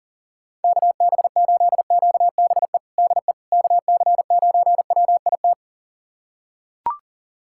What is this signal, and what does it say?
Morse code, heard as a keyed tone, about 40 wpm, 700 Hz. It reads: KB8QBE DE KC9WIT
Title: USA Callsign DE Callsign 40wpm